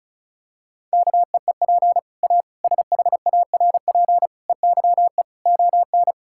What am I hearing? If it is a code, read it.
Morse code, heard as a keyed tone, about 36 wpm, 700 Hz. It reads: KEEP A SHARP EYE ON
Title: Sentences from Top 1600 Words 35wpm